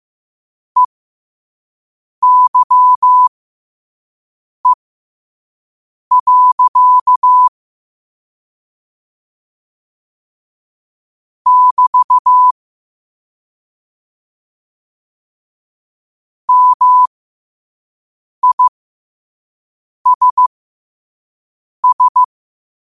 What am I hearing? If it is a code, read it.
Morse code, heard as a keyed tone, about 15 wpm, spaced as for 5 wpm, 1000 Hz. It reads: EYE. <BT> MISS
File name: qotdfiles / 5wpm